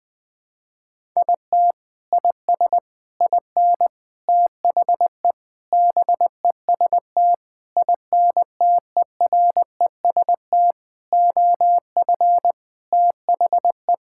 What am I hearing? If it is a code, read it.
Morse code, heard as a keyed tone, about 19 wpm, 700 Hz. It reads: IT IS IN THE BEST INTEREST OF THE
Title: Sentences with 7-Word Phrases - Spoken Beginning - 20wpm